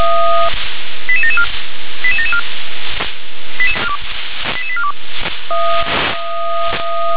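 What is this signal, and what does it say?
Five-tone selective call (ZVEI), heard as quick runs of five tones, tones of about 70 ms. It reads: 88774 88774 88842
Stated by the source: Received on an SDR as a ZVEI five-tone call.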